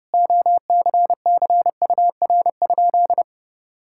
Morse code, heard as a keyed tone, about 28 wpm, 700 Hz. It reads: OCCUR?
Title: Sentences from Top 1000 Words - Review Part 6 - 30wpm